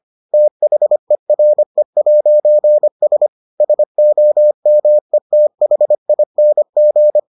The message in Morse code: THERE'S SOMETHING

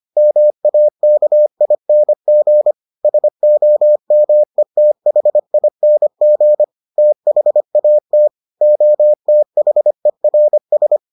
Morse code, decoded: MAKING SOMETHING THAT OTHERS

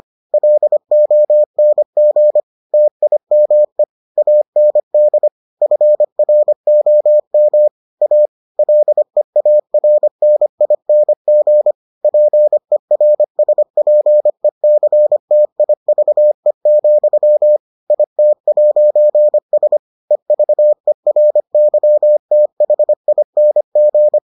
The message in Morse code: LONG TIME AND FROM A LEARNING PERSPECTIVE, IT'S EVERYTHING

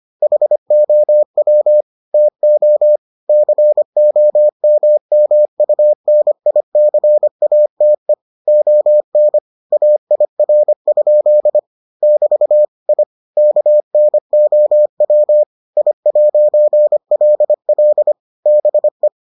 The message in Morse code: HOW TO COMMUNICATE ON AIR? <BT> I KNOW I'LL BE

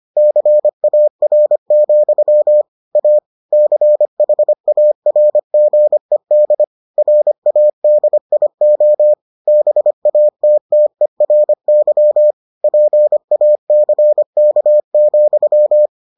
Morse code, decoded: CAR, A CHARGED RADIO BATTERY PACK,

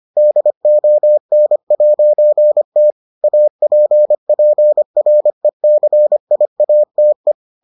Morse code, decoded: DON'T APPRECIATE

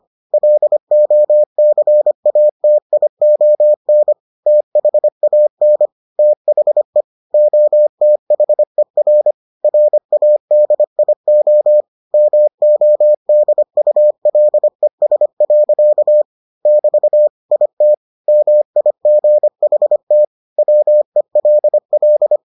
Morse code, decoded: LOCATION THAN THE OTHER RADIO MODULES. <BT> IT MIGHT WELL

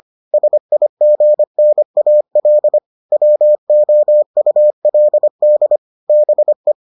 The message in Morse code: SIGNAL WOULD BE

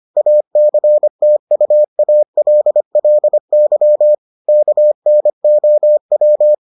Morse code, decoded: ACTUALLY KNOW

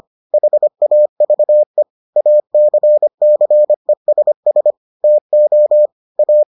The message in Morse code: HAVE ACCESS TO A

